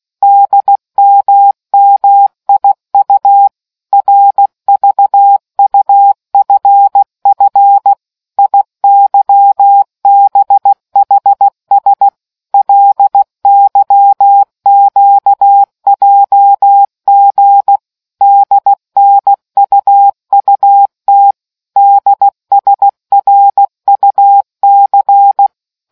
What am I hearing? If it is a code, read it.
Morse code, heard as a keyed tone, about 16 wpm, 800 Hz. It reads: DMMIU RVUFF IYBHS LYQJG DNUUT DSRUC